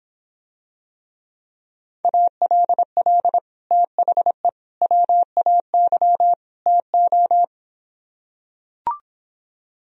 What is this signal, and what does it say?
Morse code, heard as a keyed tone, about 26 wpm, 700 Hz. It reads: ALL THE WAY TO